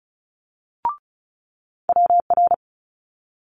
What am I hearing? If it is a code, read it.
Morse code, heard as a keyed tone, about 36 wpm, 700 Hz. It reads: WR
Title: Multi-letter Phonemes Quick Fire 35wpm